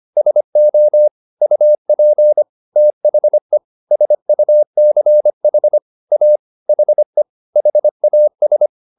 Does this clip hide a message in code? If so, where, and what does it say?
Yes: Morse code from 0.2 to 8.7 s, heard as a keyed tone, about 25 wpm, 600 Hz. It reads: SO UP THE SUCH A HE HAS